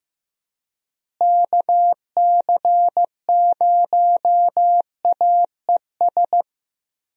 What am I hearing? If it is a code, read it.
Morse code, heard as a keyed tone, about 15 wpm, 700 Hz. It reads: KC0AES
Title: USA Call Signs - 6 Characters - Repeated 3x - 15wpm